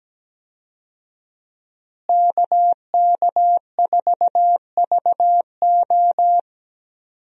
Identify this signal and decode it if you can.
Morse code, heard as a keyed tone, about 17 wpm, 700 Hz. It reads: KK4VO
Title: USA Call Signs - 5 Characters - Repeated 2x - 17wpm